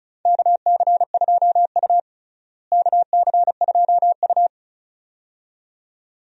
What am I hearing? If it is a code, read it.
Morse code, heard as a keyed tone, about 34 wpm, 700 Hz. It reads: KC2U KC2U
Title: USA Call Signs - 4 Characters - Repeated 2x - 35wpm